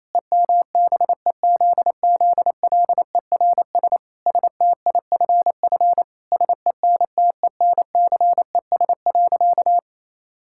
Morse code, heard as a keyed tone, about 28 wpm, 700 Hz. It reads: EMBEZZLERS STIFF SENTENCES.
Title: Pangram Challenge v2 - 28wpm